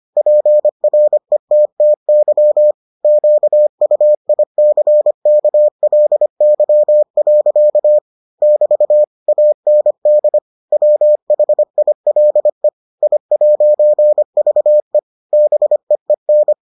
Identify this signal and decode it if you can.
Morse code, heard as a keyed tone, about 25 wpm, 600 Hz. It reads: PRETTY QUICKLY. <BT> AND WHILE I'VE BEEN